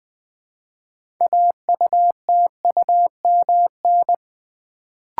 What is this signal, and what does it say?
Morse code, heard as a keyed tone, about 20 wpm, 700 Hz. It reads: AUTUMN
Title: Build A Word - 2 Letters At A Time - 6-Letter Words - 20wpm